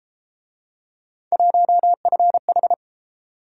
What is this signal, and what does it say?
Morse code, heard as a keyed tone, about 31 wpm, 700 Hz. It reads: 1FH